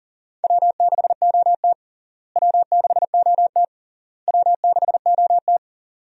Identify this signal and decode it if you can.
Morse code, heard as a keyed tone, about 40 wpm, 700 Hz. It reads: W6OT W6OT W6OT
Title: USA Call Signs - 4 Characters - Repeated 3x - 40wpm